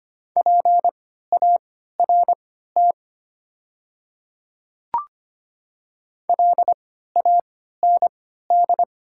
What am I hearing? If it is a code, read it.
Morse code, heard as a keyed tone, about 25 wpm, spaced as for 14 wpm, 690 Hz. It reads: PART LAND
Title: 3x Character Spacing - Top 200 Words - 25wpm